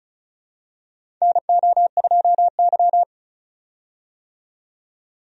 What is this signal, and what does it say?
Morse code, heard as a keyed tone, about 34 wpm, 700 Hz. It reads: NO2Y
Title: USA Call Signs - 4 Characters - 35wpm